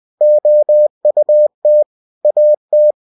Morse code, heard as a keyed tone, about 20 wpm, 600 Hz. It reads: OUT AT
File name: cw-books / winnie-the-pooh / winnie-20wpm-page-0049